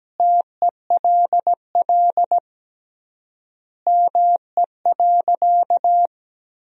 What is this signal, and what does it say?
Morse code, heard as a keyed tone, about 17 wpm, 700 Hz. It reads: TELL ME.
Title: 3x Word-Spacing - Sentences from Top 200 Words - 17wpm